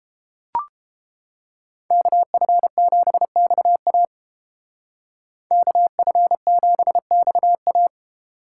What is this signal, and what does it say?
Morse code, heard as a keyed tone, about 31 wpm, 700 Hz. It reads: KF7XA KF7XA